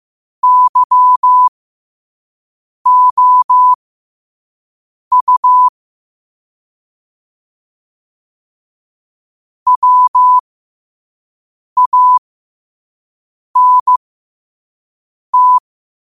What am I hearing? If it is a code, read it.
Morse code, heard as a keyed tone, about 15 wpm, spaced as for 5 wpm, 1000 Hz. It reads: YOU WANT
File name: qotdfiles / 5wpm